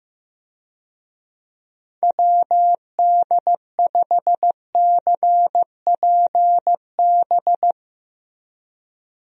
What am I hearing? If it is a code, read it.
Morse code, heard as a keyed tone, about 16 wpm, 700 Hz. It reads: WD5CPB
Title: USA Call Signs - 6 Characters - Repeated 2x - 15wpm